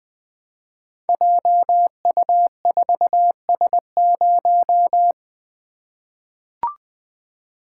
Morse code, heard as a keyed tone, about 20 wpm, 700 Hz. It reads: JU4S0
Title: Code Groups - 5 Letter:Numbers - 20wpm